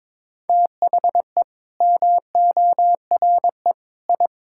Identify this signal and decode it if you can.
Morse code, heard as a keyed tone, about 22 wpm, 700 Hz. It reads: THE MORE I
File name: practice / 22wpm / Sentences with 8-Word Phrases - 22wpm